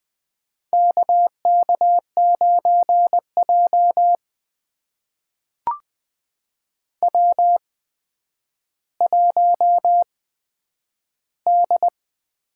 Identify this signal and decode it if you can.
Morse code, heard as a keyed tone, about 20 wpm, spaced as for 10 wpm, 700 Hz. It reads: KK9J W1D
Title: USA Call Signs - 4 Characters - 8x Character Spacing - 20wpm